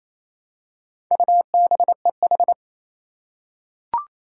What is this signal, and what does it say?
Morse code, heard as a keyed tone, about 28 wpm, 700 Hz. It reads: UBEH
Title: Four Challenging Letter-Number Combinations 28wpm